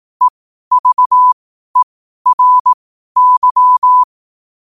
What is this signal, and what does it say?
Morse code, heard as a keyed tone, about 18 wpm, spaced as for 12 wpm, 1000 Hz. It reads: EVERY
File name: qotdfiles / 12wpm